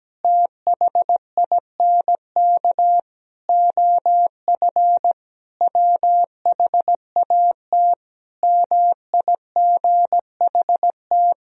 Morse code, heard as a keyed tone, about 17 wpm, 700 Hz. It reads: THINK OF WHAT MIGHT
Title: Sentences from Top 400 Words and Abbreviations 17wpm